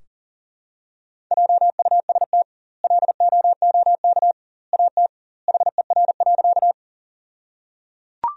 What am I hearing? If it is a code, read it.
Morse code, heard as a keyed tone, about 37 wpm, 700 Hz. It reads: JUST LOOK AT HER.
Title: 2x Word-Spacing - Sentences from Top 200 Words - 40wpm